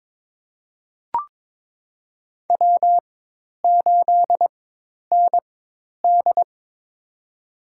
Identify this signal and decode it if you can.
Morse code, heard as a keyed tone, about 22 wpm, spaced as for 10 wpm, 700 Hz. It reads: W8ND
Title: USA Call Signs - 4x Character Spacing - 22wpm